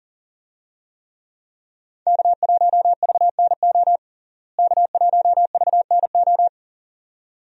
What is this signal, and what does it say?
Morse code, heard as a keyed tone, about 37 wpm, 700 Hz. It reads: K1VNO K1VNO
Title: USA Call Signs - 5 Characters - Repeated 2x - 40wpm